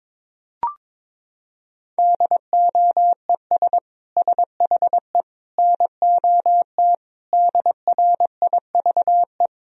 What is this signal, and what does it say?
Morse code, heard as a keyed tone, about 22 wpm, 700 Hz. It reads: DOES SHE NOT DRIVE
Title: Sentences from Top 1000 Words - Review Part 2 - 22wpm